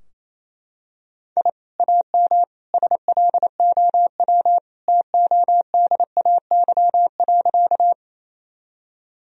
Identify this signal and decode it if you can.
Morse code, heard as a keyed tone, about 28 wpm, 700 Hz. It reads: I AM SLOW TODAY.
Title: Sentences from Top 1000 Words - Review Part 2 - 28wpm